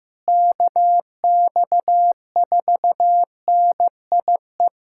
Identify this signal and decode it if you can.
Morse code, heard as a keyed tone, about 15 wpm, 700 Hz. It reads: KX4NIE
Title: USA Callsign DE Callsign - 6 Characters - 15wpm